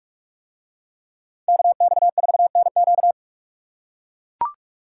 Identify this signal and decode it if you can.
Morse code, heard as a keyed tone, about 44 wpm, 700 Hz. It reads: KX4NQ